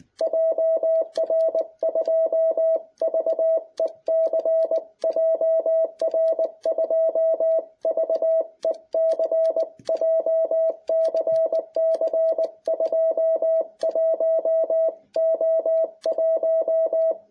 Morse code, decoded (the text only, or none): JR2VEC JR2VEC JCC 21O1